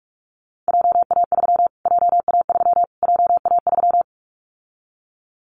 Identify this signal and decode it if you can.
Morse code, heard as a keyed tone, about 44 wpm, 700 Hz. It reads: JA3 JA3 JA3
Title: Callsign Prefixes for Top 25 DXCC - Repeated 3x - 45wpm